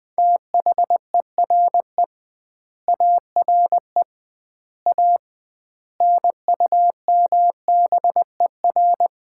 Morse code, heard as a keyed tone, about 21 wpm, 700 Hz. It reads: THERE ARE A NUMBER
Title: 2x Word-Spacing - Sets of 5 Words - 20wpm